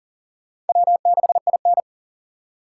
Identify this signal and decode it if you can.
Morse code, heard as a keyed tone, about 51 wpm, 700 Hz. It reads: W6IN